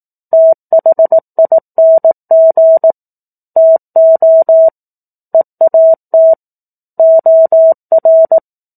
Morse code, heard as a keyed tone, about 18 wpm, 650 Hz. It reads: THING TO EAT OR
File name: 04-Chapter-04 18 WPM